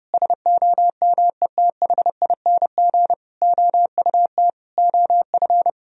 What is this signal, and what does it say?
Morse code, heard as a keyed tone, about 30 wpm, 700 Hz. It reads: SOMETHING OUT OF
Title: Sentences from Top 1000 Words - Review Part 3 - 30wpm